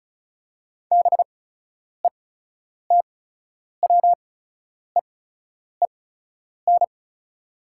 Morse code, heard as a keyed tone, about 34 wpm, spaced as for 10 wpm, 700 Hz. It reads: BETWEEN
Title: 8x Character Spacing - Top 300 Words - 35wpm